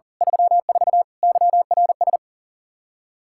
Morse code, heard as a keyed tone, about 40 wpm, 700 Hz. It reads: 34 YRS
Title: Practice QSO - Exchange 3 - More Info - 40wpm